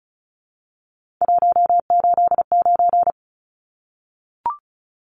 Morse code, heard as a keyed tone, about 34 wpm, 700 Hz. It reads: 189
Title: Numbers - 3-Digit 35wpm